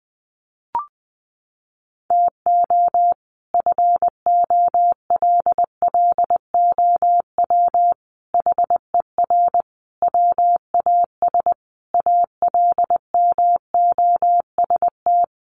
Morse code, heard as a keyed tone, about 20 wpm, 700 Hz. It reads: TO FOLLOW HER WAS ALMOST
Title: Sentences from Top 1100 Words 20wpm